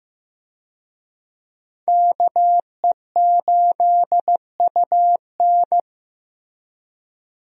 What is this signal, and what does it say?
Morse code, heard as a keyed tone, about 15 wpm, 700 Hz. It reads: KE8UN
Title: USA Call Signs - 5 Characters - Repeated 2x - 15wpm